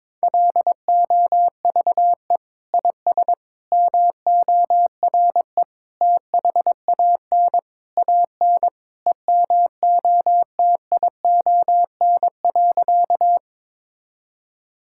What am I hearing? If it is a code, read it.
Morse code, heard as a keyed tone, about 22 wpm, 700 Hz. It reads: LOVE IS MORE THAN AN EMOTION.
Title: Sentences from Top 1600 Words 22wpm